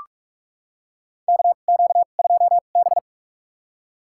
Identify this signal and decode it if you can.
Morse code, heard as a keyed tone, about 46 wpm, spaced as for 33 wpm, 700 Hz. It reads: KQ2B
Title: USA Call Signs - 4 Characters - 2x Character Spacing - 45wpm